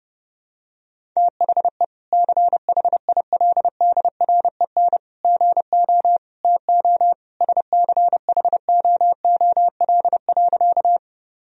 Morse code, heard as a keyed tone, about 30 wpm, 700 Hz. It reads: THE CHILDREN GO TO SCHOOL.
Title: Sentences from Top 1000 Words - Review Part 1 - 30wpm